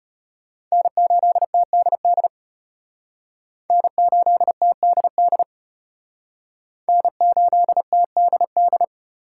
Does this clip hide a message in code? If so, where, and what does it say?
Yes: Morse code from 0.7 to 8.8 s, heard as a keyed tone, about 34 wpm, 700 Hz. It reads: N8TDD N8TDD N8TDD